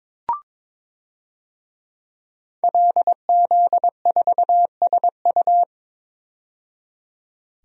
Morse code, heard as a keyed tone, about 21 wpm, 700 Hz. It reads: LZ4SU
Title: DXCC-100 Call Signs - 5 Characters - 22wpm